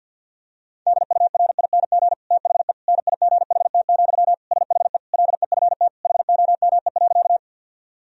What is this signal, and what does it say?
Morse code, heard as a keyed tone, about 50 wpm, 700 Hz. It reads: DURING THE NIGHT, SHE LEFT HOME.
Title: Sentences from Top 1000 Words - Review Part 2 - 50wpm